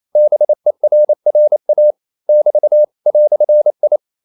Morse code, heard as a keyed tone, about 28 wpm, 600 Hz. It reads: BERRA <BT> "I
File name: CW / Practice / EnglishQuotations0128